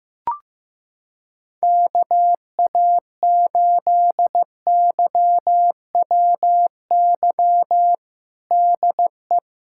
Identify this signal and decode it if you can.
Morse code, heard as a keyed tone, about 16 wpm, 700 Hz. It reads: KA8YWY DE
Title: USA Callsign DE Callsign - 6 Characters - 15wpm